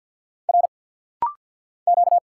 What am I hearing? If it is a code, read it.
Morse code, heard as a keyed tone, about 50 wpm, 700 Hz. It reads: R Q
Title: Single Letter-Number Alt - Rapid-Fire 50wpm